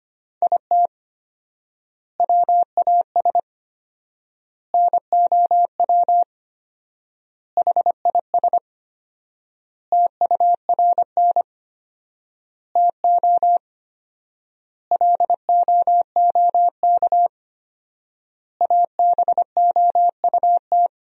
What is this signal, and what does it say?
Morse code, heard as a keyed tone, about 25 wpm, 700 Hz. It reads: IT WAS NOW HIS TURN TO LOOK ABOUT
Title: 4x Word-Spacing - Sentences from Top 200 Words - 25wpm